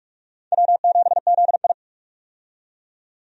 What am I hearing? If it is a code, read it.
Morse code, heard as a keyed tone, about 46 wpm, 700 Hz. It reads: W7ZI
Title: USA Call Signs - 4 Characters - 45wpm